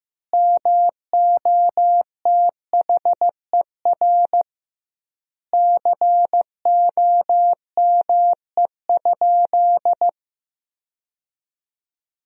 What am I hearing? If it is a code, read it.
Morse code, heard as a keyed tone, about 15 wpm, 700 Hz. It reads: MOTHER COME?
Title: 2x Word-Spacing - Sentences from Top 200 Words - 15wpm